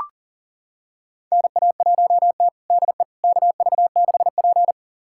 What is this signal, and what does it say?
Morse code, heard as a keyed tone, about 40 wpm, 690 Hz. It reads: NA1T DE KV6P
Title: USA Callsign DE Callsign - 4 Characters - 40wpm